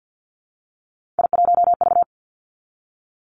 Morse code, heard as a keyed tone, about 50 wpm, 700 Hz. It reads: I1V
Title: Three Challenging Letter-Number Combinations 50wpm